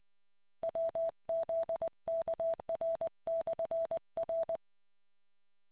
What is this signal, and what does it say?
Morse code, heard as a keyed tone, about 20 wpm, 660 Hz. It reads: WZKR/R